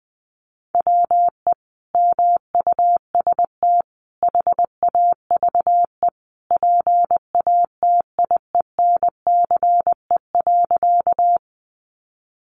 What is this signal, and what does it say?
Morse code, heard as a keyed tone, about 20 wpm, 700 Hz. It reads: WE MUST HAVE PATIENCE.
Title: Sentences from Top 1300 Words 20wpm